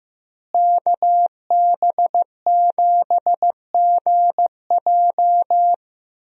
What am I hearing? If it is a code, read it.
Morse code, heard as a keyed tone, about 15 wpm, 700 Hz. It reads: KB7GJ
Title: USA Call Signs - 5 Characters - Repeated 3x - 15wpm